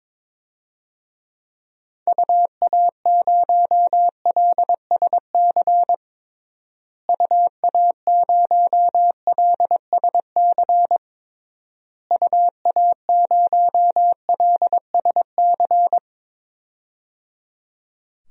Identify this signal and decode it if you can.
Morse code, heard as a keyed tone, about 22 wpm, 700 Hz. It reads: UA0LSC UA0LSC UA0LSC
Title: DXCC-100 Call Signs - 6 Characters - Repeated 3x - 22wpm